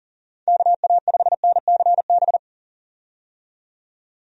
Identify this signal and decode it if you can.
Morse code, heard as a keyed tone, about 40 wpm, 700 Hz. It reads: KA5NCB
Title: USA Call Signs - 6 Characters - 40wpm